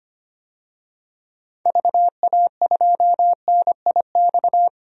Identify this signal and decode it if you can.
Morse code, heard as a keyed tone, about 25 wpm, 700 Hz. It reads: VA2NIX
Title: DXCC-100 Call Signs - 6 Characters - 25wpm